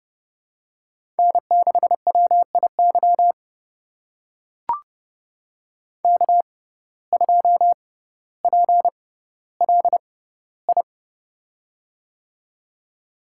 Morse code, heard as a keyed tone, about 30 wpm, spaced as for 15 wpm, 700 Hz. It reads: N6WIY K2PLI
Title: USA Call Signs - 5 Characters - 6x Character Spacing - 30wpm